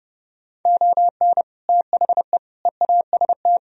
Morse code, heard as a keyed tone, about 30 wpm, 700 Hz. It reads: ON THE EAST
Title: Sentences from Top 1000 Words - Review Part 4 - 30wpm